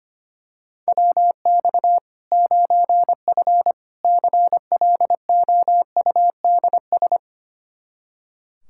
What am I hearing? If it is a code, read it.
Morse code, heard as a keyed tone, about 25 wpm, 700 Hz. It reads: WX 9F CLOUDS
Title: K1USN SST QSO Practice 25wpm